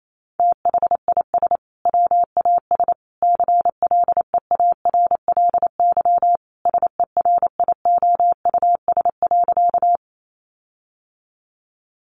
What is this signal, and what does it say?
Morse code, heard as a keyed tone, about 28 wpm, 700 Hz. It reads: THIS WAS CLEARLY SERIOUS.
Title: Sentences from Top 1300 Words 28wpm